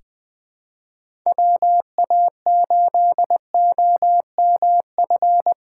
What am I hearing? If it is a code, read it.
Morse code, heard as a keyed tone, about 20 wpm, 700 Hz. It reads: WA8OMF
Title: USA Callsign DE Callsign - 6 Characters - 20wpm